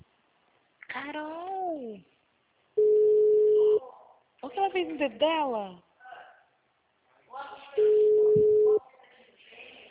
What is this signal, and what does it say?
Ringback tone (Europe, 425 Hz), with speech.
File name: out- -2001- - - .10131